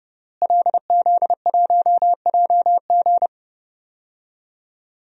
Morse code, heard as a keyed tone, about 30 wpm, 700 Hz. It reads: LZ1JG
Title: DXCC-100 Call Signs - 5 Characters - Repeated 2x - 30wpm